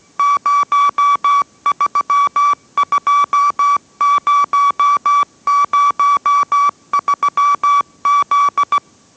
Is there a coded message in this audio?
Yes: Morse code from 0.2 to 8.8 s, heard as a keyed tone, about 17 wpm, 1150 Hz. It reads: 032003Z